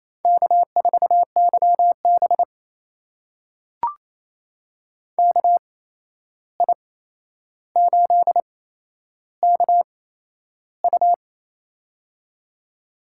Morse code, heard as a keyed tone, about 28 wpm, spaced as for 11 wpm, 700 Hz. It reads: K4YB KI8KU
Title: USA Call Signs - 8x Character Spacing - 28wpm